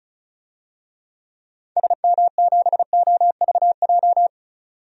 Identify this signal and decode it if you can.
Morse code, heard as a keyed tone, about 36 wpm, 700 Hz. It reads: SM7OVJ
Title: DXCC-100 Call Signs - 6 Characters - 35wpm